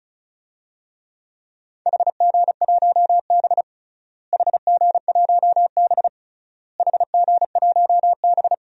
Morse code, heard as a keyed tone, about 34 wpm, 700 Hz. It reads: HG1B HG1B HG1B
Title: DXCC-100 Call Signs - 4 Characters - Repeated 3x - 35wpm